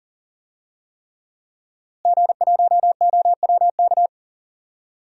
Morse code, heard as a keyed tone, about 38 wpm, 700 Hz. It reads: G1OWK
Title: DXCC-100 Call Signs - 5 Characters - Repeated 3x - 40wpm